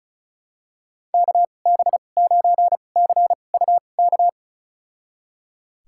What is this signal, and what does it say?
Morse code, heard as a keyed tone, about 34 wpm, spaced as for 25 wpm, 690 Hz. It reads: KB9CUK
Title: USA Call Signs - 6 Characters - 2x Character Spacing - 35wpm